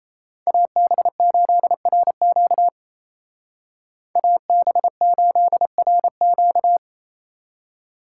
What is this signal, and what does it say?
Morse code, heard as a keyed tone, about 33 wpm, 700 Hz. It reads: AB8RQ AB8RQ
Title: Speed Racing - USA Call Signs 28wpm